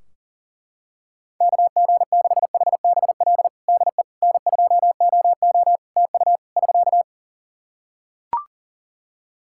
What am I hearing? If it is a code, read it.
Morse code, heard as a keyed tone, about 40 wpm, 700 Hz. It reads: KG6HBL DE N2OO TU <SK>
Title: K1USN SST QSO Practice 40wpm